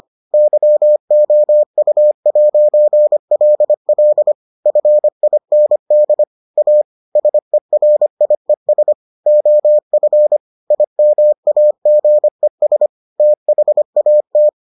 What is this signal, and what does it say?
Morse code, heard as a keyed tone, about 25 wpm, 600 Hz. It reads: YOU'LL FIND A SERIES OF IMAGES THAT